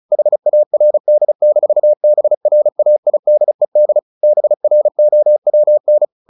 Morse code, heard as a keyed tone, about 34 wpm, 600 Hz. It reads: HARD-BRAIDED BROWN
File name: the old man and the sea / the old man and the sea 1 35wpm